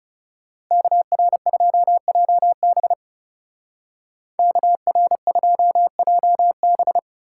Morse code, heard as a keyed tone, about 36 wpm, 700 Hz. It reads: KR2JB KR2JB
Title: Speed Racing - USA 2x2 Call Signs 30wpm